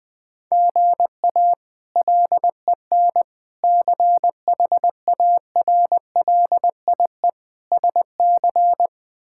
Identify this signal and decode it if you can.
Morse code, heard as a keyed tone, about 20 wpm, 700 Hz. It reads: GA LEN CHARLIE SC
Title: K1USN SST QSO Practice - 20wpm